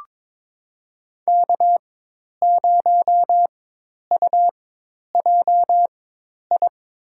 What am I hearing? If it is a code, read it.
Morse code, heard as a keyed tone, about 22 wpm, spaced as for 10 wpm, 700 Hz. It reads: K0UJI
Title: USA Call Signs - 5 Characters - 4x Character Spacing - 22wpm